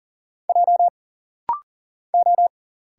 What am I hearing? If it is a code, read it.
Morse code, heard as a keyed tone, about 40 wpm, 700 Hz. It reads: J O
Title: Single Letters - Rapid-Fire 40wpm